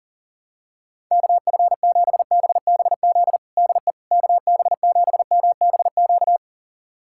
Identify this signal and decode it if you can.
Morse code, heard as a keyed tone, about 40 wpm, 700 Hz. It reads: KF7BBZ DE KB7MBQ
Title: USA Callsign DE Callsign - 6 Characters - 40wpm